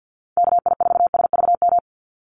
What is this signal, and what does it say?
Morse code, heard as a keyed tone, about 50 wpm, 700 Hz. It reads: KI4SVM
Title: NA SOTA Activator Callsigns 50wpm